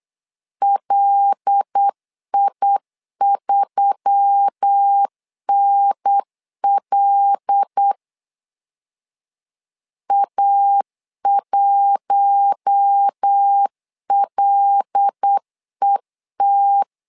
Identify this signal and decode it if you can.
Morse code, heard as a keyed tone, about 8 wpm, 800 Hz. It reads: LI3NL A1LET
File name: LEZIONI / 5 LBVC / 5 L 8wpm 2831